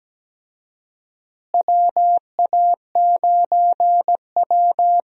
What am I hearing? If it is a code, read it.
Morse code, heard as a keyed tone, about 17 wpm, 700 Hz. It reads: WA9W
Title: USA Call Signs - 4 Characters - Repeated 3x - 17wpm